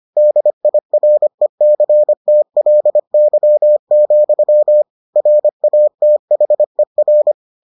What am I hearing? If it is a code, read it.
Morse code, heard as a keyed tone, about 25 wpm, 600 Hz. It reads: DIRECTLY, RATHER